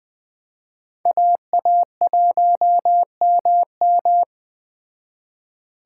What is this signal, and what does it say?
Morse code, heard as a keyed tone, about 20 wpm, 700 Hz. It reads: AA1MM
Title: Speed Racing - USA 2x2 Call Signs 17wpm